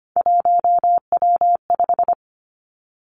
Morse code, heard as a keyed tone, about 25 wpm, 700 Hz. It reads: 1W5
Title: Three Challenging Letter-Number Combinations 25wpm